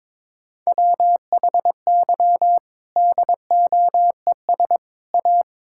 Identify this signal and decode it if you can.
Morse code, heard as a keyed tone, about 22 wpm, 700 Hz. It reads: WHY DOES A
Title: Sentences from Top 1000 Words - Review Part 5 - 22wpm